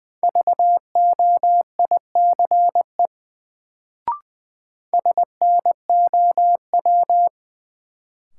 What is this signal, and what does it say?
Morse code, heard as a keyed tone, about 20 wpm, 700 Hz. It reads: VOICE SNOW
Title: Sentences from Top 500 Words - New Word Drill - 20wpm